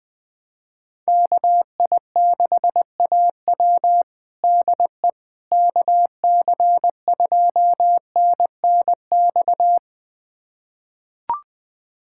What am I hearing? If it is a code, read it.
Morse code, heard as a keyed tone, about 21 wpm, 700 Hz. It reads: KI6AW DE KC2NNX
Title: USA Callsign DE Callsign 20wpm